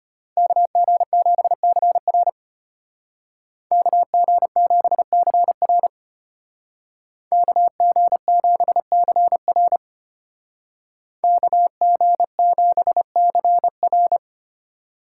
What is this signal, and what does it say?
Morse code, heard as a keyed tone, about 30 wpm, 700 Hz. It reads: KG7CR KG7CR KG7CR KG7CR
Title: Speed Racing - USA 2x2 Call Signs 25wpm